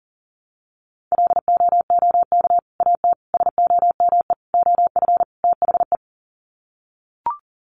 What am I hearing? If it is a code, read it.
Morse code, heard as a keyed tone, about 40 wpm, 700 Hz. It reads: LOOK AT SOME OF THE
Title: Sets of 5 Words Encore 40wpm